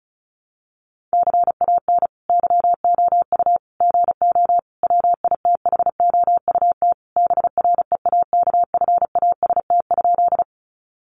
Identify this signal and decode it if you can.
Morse code, heard as a keyed tone, about 34 wpm, 700 Hz. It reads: CAN YOU GO WITHOUT BREAKFAST?
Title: Sentences from Top 1500 Words 35wpm